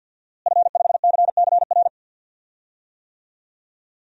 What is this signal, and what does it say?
Morse code, heard as a keyed tone, about 53 wpm, 700 Hz. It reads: F5CCR
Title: DXCC-100 Call Signs - 5 Characters - Repeated 3x - 50wpm